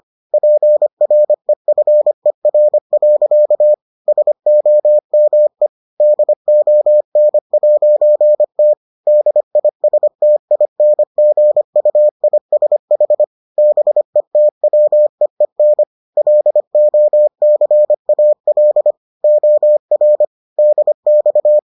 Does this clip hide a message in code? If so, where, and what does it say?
Yes: Morse code from 0.3 to 21.6 s, heard as a keyed tone, about 25 wpm, 600 Hz. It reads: PREFER. SOME DON'T DISTINGUISH BETWEEN LOCAL OR DX